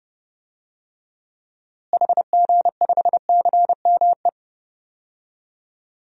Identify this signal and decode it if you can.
Morse code, heard as a keyed tone, about 32 wpm, 700 Hz. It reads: HG5CME
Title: DXCC-100 Call Signs - 6 Characters - Repeated 3x - 30wpm